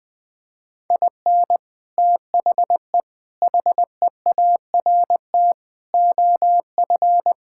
Morse code, heard as a keyed tone, about 19 wpm, 700 Hz. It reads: IN THE HEART OF
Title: Sentences with 5-Word Phrases - Spoken Beginning - 20wpm